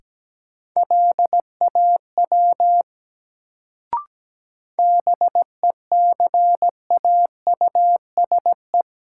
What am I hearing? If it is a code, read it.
Morse code, heard as a keyed tone, about 17 wpm, 700 Hz. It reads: LAW BECAUSE
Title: Sentences from Top 600 Words - New Word Drill - 17wpm